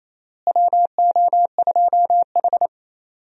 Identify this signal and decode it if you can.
Morse code, heard as a keyed tone, about 28 wpm, 700 Hz. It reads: WO2H
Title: USA Call Signs - 4 Characters - Repeated 3x - 28wpm